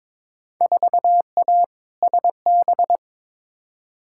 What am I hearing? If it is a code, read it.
Morse code, heard as a keyed tone, about 21 wpm, 700 Hz. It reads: 4A SB
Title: Field Day Exchanges Encore - 22wpm